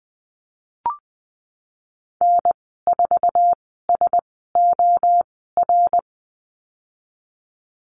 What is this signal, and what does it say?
Morse code, heard as a keyed tone, about 20 wpm, spaced as for 15 wpm, 700 Hz. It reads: N4SOR
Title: 076 Lesson 13g - Letter L - Call Signs 20wpm